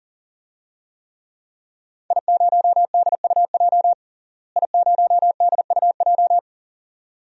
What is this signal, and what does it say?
Morse code, heard as a keyed tone, about 40 wpm, 700 Hz. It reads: I0DUJ I0DUJ
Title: DXCC-100 Call Signs - 5 Characters - Repeated 2x - 40wpm